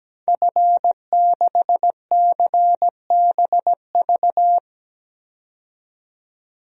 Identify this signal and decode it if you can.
Morse code, heard as a keyed tone, about 17 wpm, 700 Hz. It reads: F6CBV
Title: DXCC-100 Call Signs - 5 Characters - 17wpm